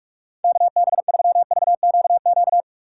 Morse code, heard as a keyed tone, about 42 wpm, 690 Hz. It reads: KB3VQQ
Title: USA Call Signs - 6x Character Spacing - 45wpm